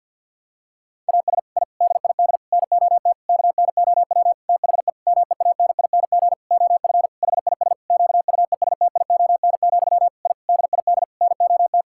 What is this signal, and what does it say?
Morse code, heard as a keyed tone, about 47 wpm, 700 Hz. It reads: AS I DID NOT KNOW THE MEANING OF HIS QUESTION, I DID NOT
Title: Sentences from Top 1000 Words - Review Part 6 - 50wpm